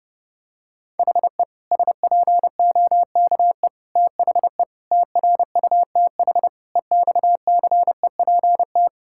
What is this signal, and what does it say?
Morse code, heard as a keyed tone, about 32 wpm, 700 Hz. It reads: HE SPOKE THE TRUTH EXCEPT
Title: Sentences from Top 1600 Words 30wpm